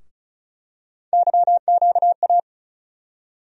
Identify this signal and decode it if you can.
Morse code, heard as a keyed tone, about 36 wpm, 700 Hz. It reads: YQA
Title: USA Call Sign Postfixes - 3 Letters 35wpm